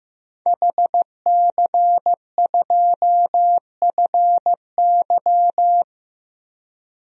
Morse code, heard as a keyed tone, about 15 wpm, 700 Hz. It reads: HC2FY
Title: DXCC-100 Call Signs - 5 Characters - Repeated 2x - 15wpm